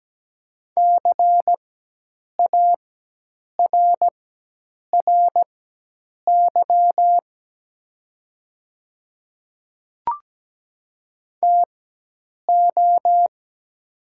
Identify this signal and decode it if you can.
Morse code, heard as a keyed tone, about 17 wpm, spaced as for 8 wpm, 700 Hz. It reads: CARRY TO
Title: 4x Character Spacing - Top 300 Words - 17wpm